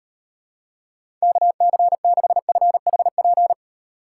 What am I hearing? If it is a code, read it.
Morse code, heard as a keyed tone, about 38 wpm, 700 Hz. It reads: KC6FHP